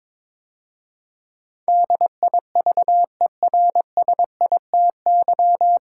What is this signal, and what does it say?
Morse code, heard as a keyed tone, about 22 wpm, 690 Hz. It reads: DIVERSITY
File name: 9 Letter Words - First 3 Letters Given - 22wpm